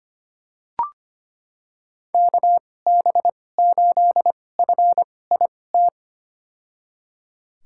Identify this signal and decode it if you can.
Morse code, heard as a keyed tone, about 25 wpm, spaced as for 18 wpm, 700 Hz. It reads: KB8FIT
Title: USA Call Signs - 2x Character Spacing - 25wpm